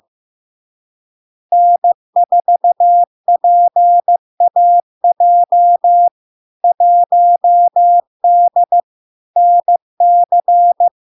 Morse code, heard as a keyed tone, about 15 wpm, 700 Hz. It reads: N4PAJ 1D NC
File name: LARK-FD-2022-Log-in-CW-15wpm-700-Hz